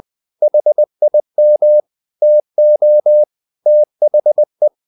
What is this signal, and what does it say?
Morse code, heard as a keyed tone, about 20 wpm, 600 Hz. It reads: HIM TO THE
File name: the old man and the sea part 2 20wpm